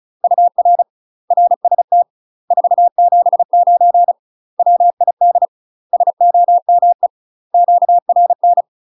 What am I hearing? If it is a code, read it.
Morse code, heard as a keyed tone, about 34 wpm, 700 Hz. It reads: UR RST 479 WID SOME QRN
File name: UK MASTER QSOs TRAINING / 35WPM / UK MASTER QSOs 0042